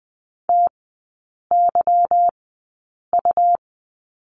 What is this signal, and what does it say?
Morse code, heard as a keyed tone, about 20 wpm, spaced as for 8 wpm, 700 Hz. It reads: TYU
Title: 178 Lesson 29e - Slash - Three Characters 20wpm